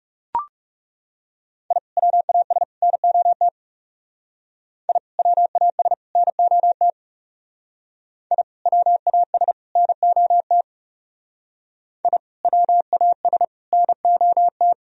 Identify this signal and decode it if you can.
Morse code, heard as a keyed tone, about 38 wpm, 700 Hz. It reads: I WAS NOT I WAS NOT I WAS NOT I WAS NOT